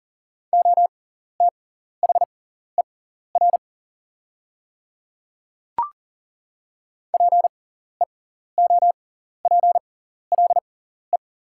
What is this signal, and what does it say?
Morse code, heard as a keyed tone, about 40 wpm, spaced as for 14 wpm, 690 Hz. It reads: OTHER PEOPLE
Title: 6x Character Spacing - Top 100 Words - 40wpm